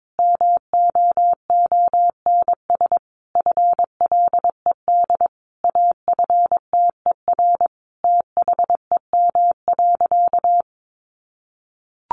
Morse code, heard as a keyed tone, about 22 wpm, 700 Hz. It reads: MOONS FLED AFTER THEM.
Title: A Sound Of Thunder 22wpm